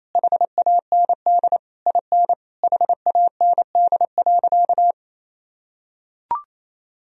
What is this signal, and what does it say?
Morse code, heard as a keyed tone, about 28 wpm, 700 Hz. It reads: HAND IN HAND.
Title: Sentences from Top 1000 Words - Review Part 4 - 28wpm